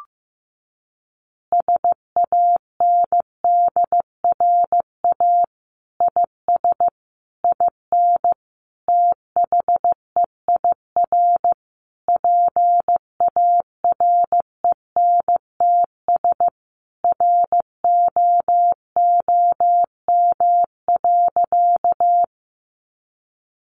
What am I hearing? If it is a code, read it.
Morse code, heard as a keyed tone, about 15 wpm, 700 Hz. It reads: SANDRA IS IN THEIR PARENTS ROOM.
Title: Easy Phrases 15wpm